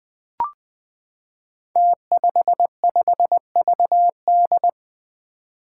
Morse code, heard as a keyed tone, about 20 wpm, 700 Hz. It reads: T55VD
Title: Five Challenging Letter-Number Combinations 20wpm